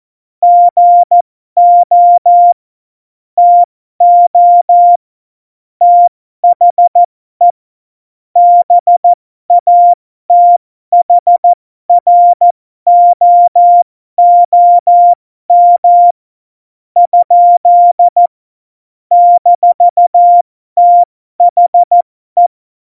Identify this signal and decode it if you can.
Morse code, heard as a keyed tone, about 14 wpm, spaced as for 12 wpm, 700 Hz. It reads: GO TO THE BATHROOM? - THE
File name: Morsekurs / Witze